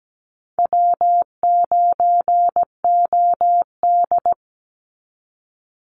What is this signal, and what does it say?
Morse code, heard as a keyed tone, about 17 wpm, 700 Hz. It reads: W9OD
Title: Easy Call Signs Repeated 3x - 17wpm